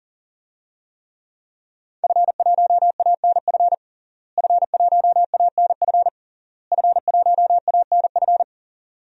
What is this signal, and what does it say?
Morse code, heard as a keyed tone, about 40 wpm, 700 Hz. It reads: F1ANF F1ANF F1ANF
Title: DXCC-100 Call Signs - 5 Characters - Repeated 3x - 40wpm